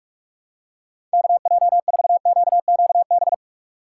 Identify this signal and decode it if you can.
Morse code, heard as a keyed tone, about 42 wpm, 690 Hz. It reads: KJ4QQB
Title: USA Call Signs - 6 Characters - 3x Character Spacing - 45wpm